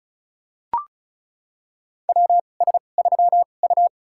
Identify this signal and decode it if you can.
Morse code, heard as a keyed tone, about 34 wpm, spaced as for 25 wpm, 700 Hz. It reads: WS3U
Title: USA Call Signs - 4 Characters - 2x Character Spacing - 35wpm